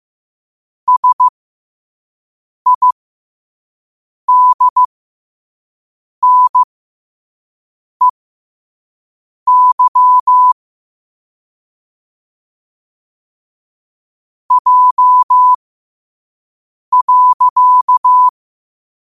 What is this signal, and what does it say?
Morse code, heard as a keyed tone, about 15 wpm, spaced as for 5 wpm, 1000 Hz. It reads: SIDNEY J.
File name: qotdfiles / 5wpm